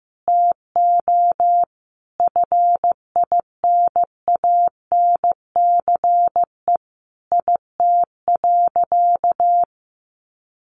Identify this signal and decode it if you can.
Morse code, heard as a keyed tone, about 15 wpm, 700 Hz. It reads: TO FINANCE IT.
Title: Sentences from Top 1500 Words 15wpm